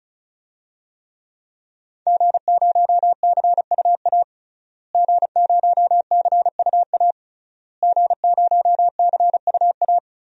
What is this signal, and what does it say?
Morse code, heard as a keyed tone, about 34 wpm, 700 Hz. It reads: G0CUA G0CUA G0CUA
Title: DXCC-100 Call Signs - 5 Characters - Repeated 3x - 35wpm